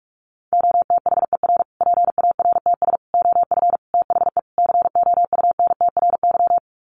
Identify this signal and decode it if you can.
Morse code, heard as a keyed tone, about 44 wpm, 700 Hz. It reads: OTHER PARTS OF THE COUNTRY
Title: Sets of 5 Words Encore 45wpm